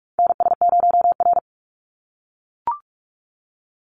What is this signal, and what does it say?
Morse code, heard as a keyed tone, about 46 wpm, 700 Hz. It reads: NS0R
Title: Call Signs - 45wpm